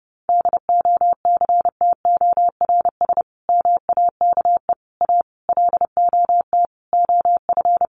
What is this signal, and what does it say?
Morse code, heard as a keyed tone, about 30 wpm, 700 Hz. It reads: DOCTORS MAKE A LOT OF
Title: Easy Phrases 30wpm